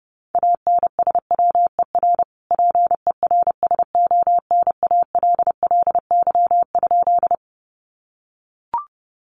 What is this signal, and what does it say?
Morse code, heard as a keyed tone, about 30 wpm, 700 Hz. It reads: ANSWER PERSONALLY?
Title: Sentences from Top 1400 Words 30wpm